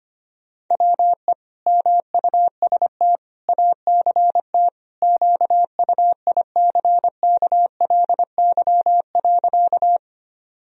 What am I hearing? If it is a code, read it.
Morse code, heard as a keyed tone, about 25 wpm, 700 Hz. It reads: WE MUST ACT QUICKLY.
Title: Sentences from Top 1000 Words - Review Part 1 - 25wpm